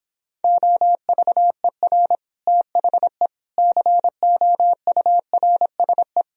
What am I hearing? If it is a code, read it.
Morse code, heard as a keyed tone, about 29 wpm, 700 Hz. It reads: OVER THE COURSE